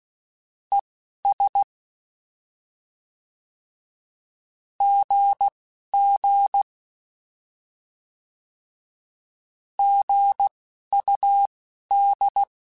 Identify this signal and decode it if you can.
Morse code, heard as a keyed tone, about 16 wpm, spaced as for 11 wpm, 800 Hz. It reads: ES GG GUD
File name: Short Runs / Short Runs-mp3s / Words @10